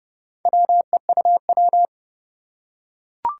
Morse code, heard as a keyed tone, about 30 wpm, 700 Hz. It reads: WEUW
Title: Four Challenging Letter-Number Combinations 30wpm